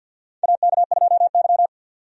Morse code, heard as a keyed tone, about 47 wpm, 700 Hz. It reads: AK2Y
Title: USA Call Signs - 4 Characters - 50wpm